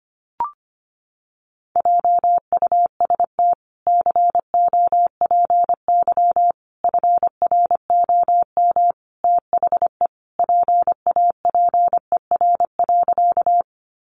Morse code, heard as a keyed tone, about 25 wpm, 700 Hz. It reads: JUST COPY FROM THE PAPER.
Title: Sentences from Top 1400 Words 25wpm